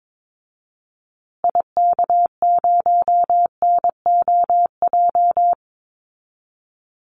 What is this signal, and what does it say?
Morse code, heard as a keyed tone, about 22 wpm, 700 Hz. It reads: IK0NOJ
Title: Call Signs Repeated 2x - 22wpm